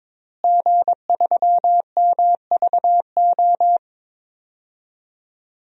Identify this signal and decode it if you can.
Morse code, heard as a keyed tone, about 22 wpm, 700 Hz. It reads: G3MVO
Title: DXCC-100 Call Signs - 5 Characters - Repeated 2x - 22wpm